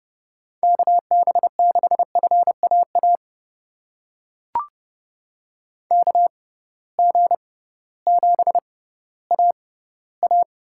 Morse code, heard as a keyed tone, about 30 wpm, spaced as for 16 wpm, 700 Hz. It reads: KB6FAA KG7AA
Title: USA Call Signs - 6 Characters - 6x Character Spacing - 30wpm